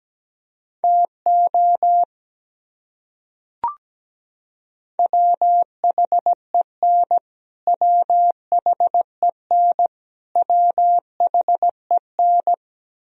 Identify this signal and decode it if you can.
Morse code, heard as a keyed tone, about 17 wpm, 700 Hz. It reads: TO WHEN WHEN WHEN
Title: Top 100 Words Repeated 17wpm